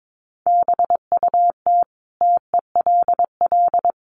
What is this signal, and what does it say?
Morse code, heard as a keyed tone, about 22 wpm, 700 Hz. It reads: BUT TELL
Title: Sentences from Top 1500 Words 22wpm